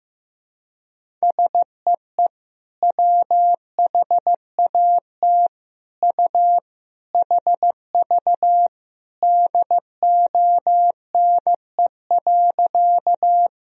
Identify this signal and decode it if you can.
Morse code, heard as a keyed tone, about 15 wpm, 700 Hz. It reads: SEE WHAT U HV DONE.
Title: Sentences from Top 500 Words and Abbreviations 15wpm